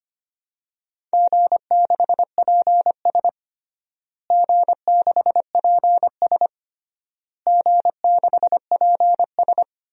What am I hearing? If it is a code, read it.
Morse code, heard as a keyed tone, about 25 wpm, 700 Hz. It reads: G6PS G6PS G6PS
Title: DXCC-100 Call Signs - 4 Characters - Repeated 3x - 25wpm